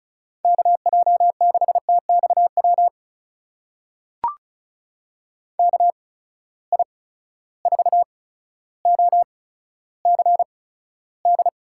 Morse code, heard as a keyed tone, about 36 wpm, spaced as for 15 wpm, 700 Hz. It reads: KJ6TXW KI4OCD
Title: USA Call Signs - 6 Characters - 8x Character Spacing - 35wpm